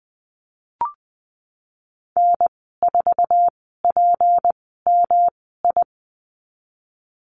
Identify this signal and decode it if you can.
Morse code, heard as a keyed tone, about 20 wpm, spaced as for 15 wpm, 700 Hz. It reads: N4PMI
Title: 154 Lesson 25g - Letter p - Call Signs 20wpm